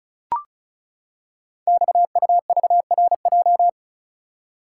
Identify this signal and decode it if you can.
Morse code, heard as a keyed tone, about 36 wpm, 690 Hz. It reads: XUVRJ
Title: Code Groups - 5 Letters - 35wpm